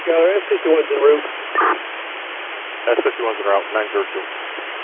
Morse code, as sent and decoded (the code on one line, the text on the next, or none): none